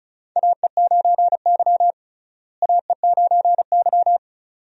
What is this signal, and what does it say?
Morse code, heard as a keyed tone, about 34 wpm, 700 Hz. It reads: AE9Y AE9Y
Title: USA Call Signs - 4 Characters - Repeated 2x - 35wpm